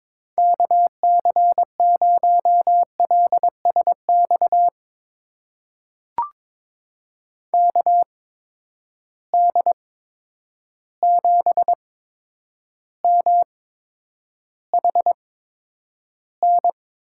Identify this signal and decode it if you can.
Morse code, heard as a keyed tone, about 22 wpm, spaced as for 9 wpm, 700 Hz. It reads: KC0LSX KD7MHN
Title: USA Call Signs - 6 Characters - 8x Character Spacing - 22wpm